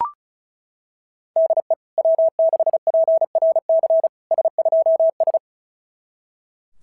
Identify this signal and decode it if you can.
Morse code, heard as a keyed tone, about 36 wpm, 660 Hz. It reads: DE W6PRC S2S
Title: SOTA QSO Practice 35wpm